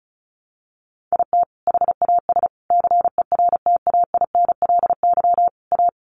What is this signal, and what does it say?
Morse code, heard as a keyed tone, about 34 wpm, 700 Hz. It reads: IT HAS CERTAINLY A
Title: Sentences from Top 1200 Words 35wpm